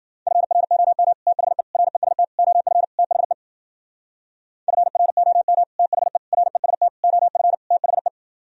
Morse code, heard as a keyed tone, about 53 wpm, 700 Hz. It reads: FROM THE REST OF THE FROM THE REST OF THE